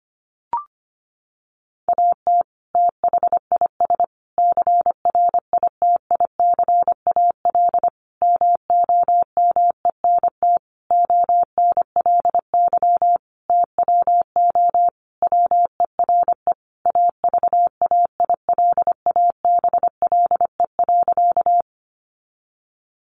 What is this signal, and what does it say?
Morse code, heard as a keyed tone, about 25 wpm, 700 Hz. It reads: AT THIS CRITICAL MOMENT ONLY TWO WERE AVAILABLE.
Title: Sentences from Top 1100 Words 25wpm